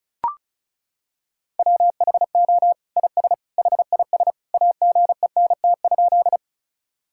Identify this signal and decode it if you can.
Morse code, heard as a keyed tone, about 36 wpm, 700 Hz. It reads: WHO IS HIS AGENT?
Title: Sentences from Top 1600 Words 35wpm